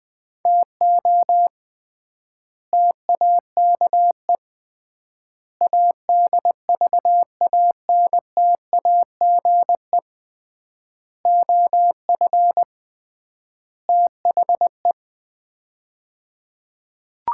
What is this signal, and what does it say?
Morse code, heard as a keyed tone, about 20 wpm, 700 Hz. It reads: TO TAKE ADVANTAGE OF THE
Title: 3x Word-Spacing - Sets of 5 Words - 20wpm